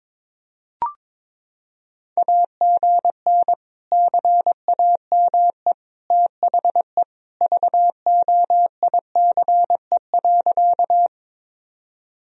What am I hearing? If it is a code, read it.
Morse code, heard as a keyed tone, about 22 wpm, 700 Hz. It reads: AGN CAME THE VOICE.
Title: Sentences from Top 500 Words and Abbreviations 22wpm